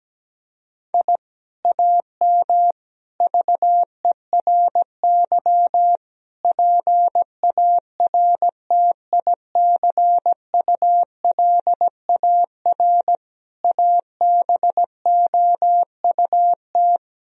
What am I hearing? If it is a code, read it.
Morse code, heard as a keyed tone, about 17 wpm, 700 Hz. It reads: I AM VERY PARTICULAR ABOUT
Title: Sentences from Top 1000 Words - Review Part 4 - 17wpm